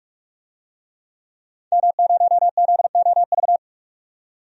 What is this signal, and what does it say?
Morse code, heard as a keyed tone, about 47 wpm, 700 Hz. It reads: M0ZOV
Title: DXCC-100 Call Signs - 5 Characters - 45wpm